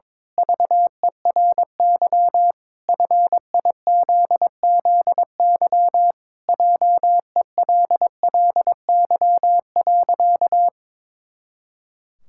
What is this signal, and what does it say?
Morse code, heard as a keyed tone, about 22 wpm, 700 Hz. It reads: VERY FIZZY JELLY.
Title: Pangram Challenge v2 - 22wpm